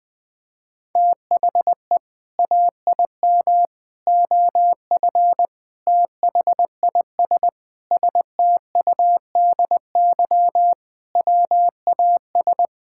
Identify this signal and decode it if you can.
Morse code, heard as a keyed tone, about 20 wpm, 700 Hz. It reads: THE AIM OF THIS STUDY WAS
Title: Sentences with 7-Word Phrases - 20wpm